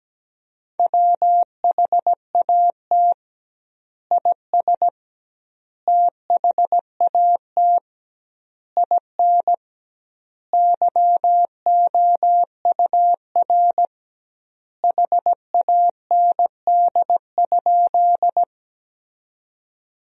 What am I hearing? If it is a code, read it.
Morse code, heard as a keyed tone, about 17 wpm, 700 Hz. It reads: WHAT IS THAT IN YOUR HAND?
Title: 2x Word-Spacing - Sentences from Top 200 Words - 17wpm